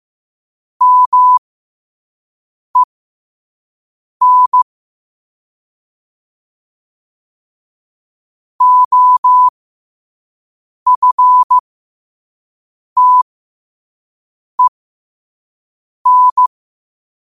Morse code, heard as a keyed tone, about 15 wpm, spaced as for 5 wpm, 1000 Hz. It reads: MEN OFTEN
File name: qotdfiles / 5wpm